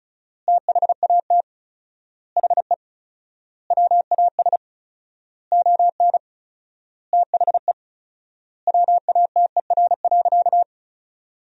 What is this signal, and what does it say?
Morse code, heard as a keyed tone, about 36 wpm, 700 Hz. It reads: THAT HE WAS ON THE WATER.
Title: 4x Word-Spacing - Sentences from Top 500 Words - 35wpm